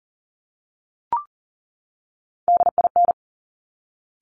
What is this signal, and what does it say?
Morse code, heard as a keyed tone, about 41 wpm, 700 Hz. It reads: DIN
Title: Top Three Letter Combinations in Words 40wpm